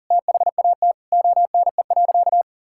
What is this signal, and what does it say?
Morse code, heard as a keyed tone, about 37 wpm, 700 Hz. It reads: THAT ONE.
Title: Sentences from Top 1600 Words 40wpm